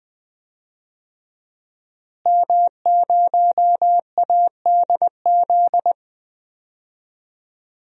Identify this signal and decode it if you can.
Morse code, heard as a keyed tone, about 20 wpm, 700 Hz. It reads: M0ADZ
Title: DXCC-100 Call Signs - 5 Characters - Repeated 3x - 20wpm